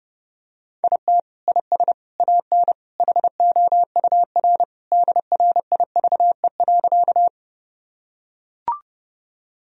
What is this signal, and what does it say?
Morse code, heard as a keyed tone, about 30 wpm, 700 Hz. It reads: IT IS AN HOUR DRIVE.
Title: Sentences from Top 1000 Words - Review Part 2 - 30wpm